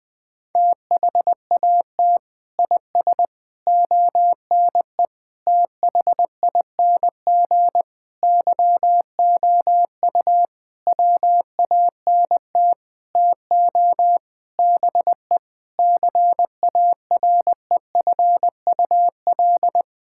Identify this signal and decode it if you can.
Morse code, heard as a keyed tone, about 20 wpm, 700 Hz. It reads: THAT IS ONE THING YOU WANT TO BE CAREFUL
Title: Sentences from Top 1600 Words 20wpm